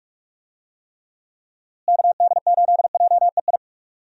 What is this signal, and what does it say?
Morse code, heard as a keyed tone, about 46 wpm, 700 Hz. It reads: KD8JEI
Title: USA Call Signs - 6 Characters - Repeated 3x - 45wpm